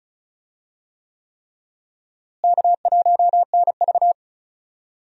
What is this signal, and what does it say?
Morse code, heard as a keyed tone, about 34 wpm, 700 Hz. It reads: K1NV
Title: USA Call Signs - 4 Characters - Repeated 3x - 35wpm